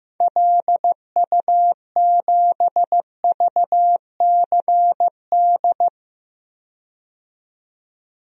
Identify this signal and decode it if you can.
Morse code, heard as a keyed tone, about 16 wpm, 700 Hz. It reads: LU7VCD
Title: DXCC-100 Call Signs - 6 Characters - 15wpm